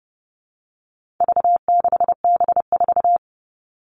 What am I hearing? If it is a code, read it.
Morse code, heard as a keyed tone, about 30 wpm, 700 Hz. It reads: V6B4
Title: All Cut Numbers - 4-Digit 30wpm